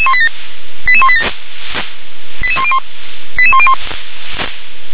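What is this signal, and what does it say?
Five-tone selective call (ZVEI), heard as quick runs of five tones, tones of about 70 ms. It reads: 88187 88101 88101
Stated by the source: ZVEI five-tone selective call received on SDR.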